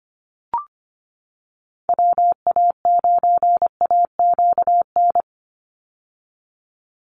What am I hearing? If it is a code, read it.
Morse code, heard as a keyed tone, about 25 wpm, 700 Hz. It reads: WA9AQN
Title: Call Signs - 25wpm